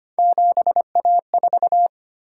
Morse code, heard as a keyed tone, about 25 wpm, 700 Hz. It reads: 7A4
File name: Three Letter-Number Combinations - 25wpm